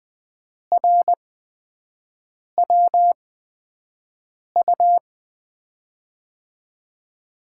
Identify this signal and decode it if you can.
Morse code, heard as a keyed tone, about 20 wpm, spaced as for 5 wpm, 700 Hz. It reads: RWU
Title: USA Call Signs - 6 Characters - 8x Character Spacing - 20wpm